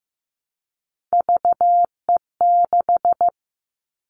Morse code, heard as a keyed tone, about 15 wpm, 700 Hz. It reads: VE6
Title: Callsign Prefixes for Top 50 DXCC - 15wpm